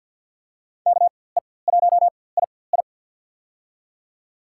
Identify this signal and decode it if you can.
Morse code, heard as a keyed tone, about 47 wpm, spaced as for 23 wpm, 700 Hz. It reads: KE1II
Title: USA Call Signs - 5 Characters - 4x Character Spacing - 50wpm